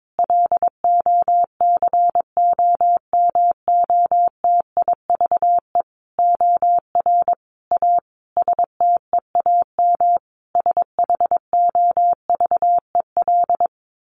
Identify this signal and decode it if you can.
Morse code, heard as a keyed tone, about 22 wpm, 700 Hz. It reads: LOCOMOTIVE OR A STEAM SHOVEL
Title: A Sound Of Thunder 22wpm